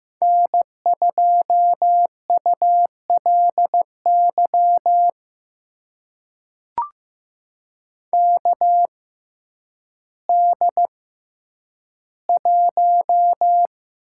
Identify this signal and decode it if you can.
Morse code, heard as a keyed tone, about 15 wpm, spaced as for 9 wpm, 700 Hz. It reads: N2ULY KD1
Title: USA Call Signs - 5 Characters - 6x Character Spacing - 15wpm